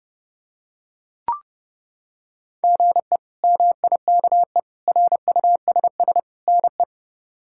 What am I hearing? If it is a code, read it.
Morse code, heard as a keyed tone, about 32 wpm, 700 Hz. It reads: GE MIKE RUSS NE
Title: K1USN SST QSO Practice - 30wpm